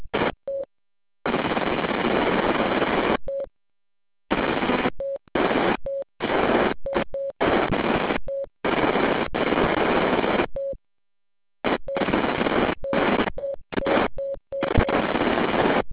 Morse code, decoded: none